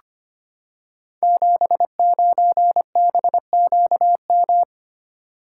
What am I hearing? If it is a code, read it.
Morse code, heard as a keyed tone, about 25 wpm, 700 Hz. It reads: 79BQM
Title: Code Groups - 5 Letter:Numbers - 25wpm